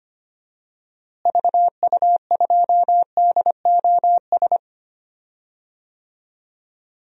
Morse code, heard as a keyed tone, about 25 wpm, 700 Hz. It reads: VU2DOS
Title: DXCC-100 Call Signs - 6 Characters - Repeated 2x - 25wpm